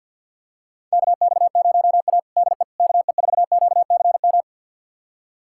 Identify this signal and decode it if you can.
Morse code, heard as a keyed tone, about 47 wpm, 700 Hz. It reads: KX0A DE KE4QCM
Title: USA Callsign DE Callsign 50wpm